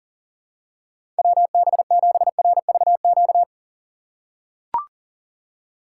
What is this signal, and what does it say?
Morse code, heard as a keyed tone, about 40 wpm, 700 Hz. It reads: WB7RVQ
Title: USA Call Signs - 4x Character Spacing - 40wpm